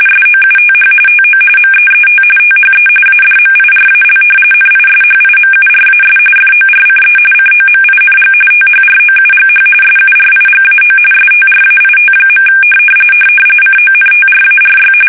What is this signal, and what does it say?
RTTY, about 45.45 baud. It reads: SOUTH TO MEET THE ADMIRAL NISHIMURAS FORCE COMING THROUGH SURIGAO STRAIT. CONTRARY
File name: fsk850 / battleshipnj